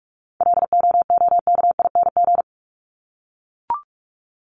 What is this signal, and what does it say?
Morse code, heard as a keyed tone, about 44 wpm, 700 Hz. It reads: LOOKING
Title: ING Suffix - Top 100 Words with Base Words 45wpm